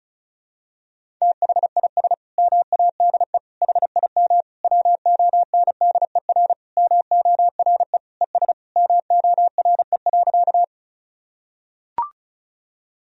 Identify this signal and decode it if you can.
Morse code, heard as a keyed tone, about 34 wpm, 700 Hz. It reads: THIS MADE HIM WONDER MORE ES MORE.
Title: Sentences from Top 500 Words and Abbreviations 35wpm